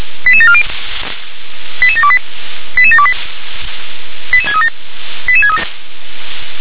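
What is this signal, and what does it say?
Five-tone selective call (ZVEI), heard as quick runs of five tones, tones of about 70 ms. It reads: 88740 88728 88728 88637
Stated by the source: ZVEI five-tone selective call 88730, received on SDR.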